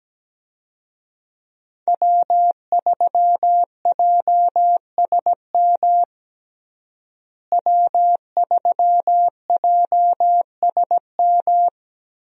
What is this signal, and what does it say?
Morse code, heard as a keyed tone, about 17 wpm, 700 Hz. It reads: W3JSM W3JSM
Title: USA Call Signs - 5 Characters - Repeated 3x - 17wpm